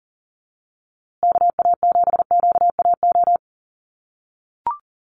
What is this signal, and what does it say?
Morse code, heard as a keyed tone, about 40 wpm, 700 Hz. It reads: KA7QAO
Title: USA Call Signs Repeated 2x 40wpm